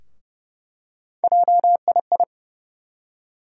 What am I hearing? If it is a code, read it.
Morse code, heard as a keyed tone, about 31 wpm, 700 Hz. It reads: JII
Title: USA Call Sign Postfixes - 3 Letters 30wpm